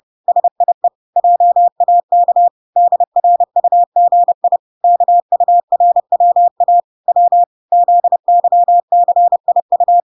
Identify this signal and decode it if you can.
Morse code, heard as a keyed tone, about 30 wpm, 700 Hz. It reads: SIE JAK DRUGI KURWA W ZYCIU